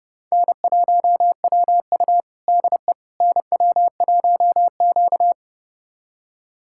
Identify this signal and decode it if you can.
Morse code, heard as a keyed tone, about 29 wpm, 700 Hz. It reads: N1WU DE NW1Q
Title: USA Callsign DE Callsign - 4 Characters - 30wpm